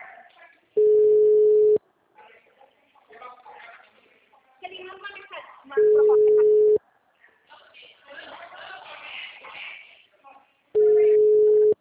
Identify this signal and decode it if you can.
Ringback tone (Europe, 425 Hz), with speech.